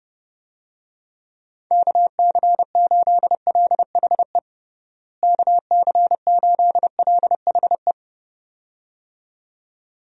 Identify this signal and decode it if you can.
Morse code, heard as a keyed tone, about 30 wpm, 700 Hz. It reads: KC8LHE KC8LHE
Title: USA Call Signs - 6 Characters - Repeated 2x - 30wpm